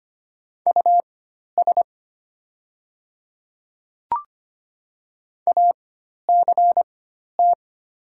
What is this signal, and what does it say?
Morse code, heard as a keyed tone, about 25 wpm, spaced as for 12 wpm, 700 Hz. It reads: US ACT
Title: 4x Character Spacing - Top 300 Words - 25wpm